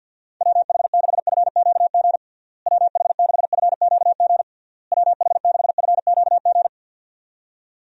Morse code, heard as a keyed tone, about 51 wpm, 700 Hz. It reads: WH6FQG WH6FQG WH6FQG
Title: USA Call Signs - 6 Characters - Repeated 3x - 50wpm